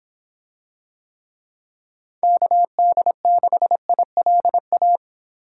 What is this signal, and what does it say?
Morse code, heard as a keyed tone, about 25 wpm, 700 Hz. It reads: KD6ILA